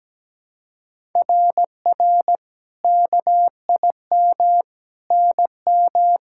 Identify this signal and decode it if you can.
Morse code, heard as a keyed tone, about 17 wpm, 700 Hz. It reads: RR KIM NM
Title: SOTA QSO Practice 17wpm